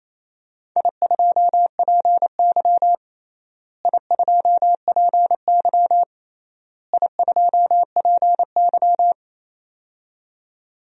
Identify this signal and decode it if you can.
Morse code, heard as a keyed tone, about 28 wpm, 700 Hz. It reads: I2PY I2PY I2PY
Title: DXCC-100 Call Signs - 4 Characters - Repeated 3x - 28wpm